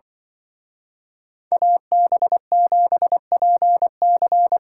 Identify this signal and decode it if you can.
Morse code, heard as a keyed tone, about 24 wpm, 700 Hz. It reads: AB7PC
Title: Speed Racing - USA 2x2 Call Signs 20wpm